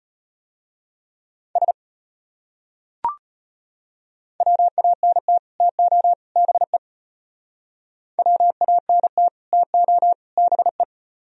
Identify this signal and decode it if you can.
Morse code, heard as a keyed tone, about 31 wpm, 700 Hz. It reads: S WANT TO BE WANT TO BE
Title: Speed-Racing - Sets of 3 Words 25wpm